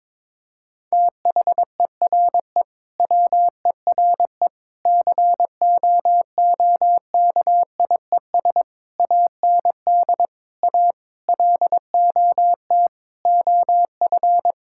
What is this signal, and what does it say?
Morse code, heard as a keyed tone, about 22 wpm, 700 Hz. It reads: THERE WERE COOKIES AND A LOT OF
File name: practice / 22wpm / Sentences with 4-Word Phrases - 22wpm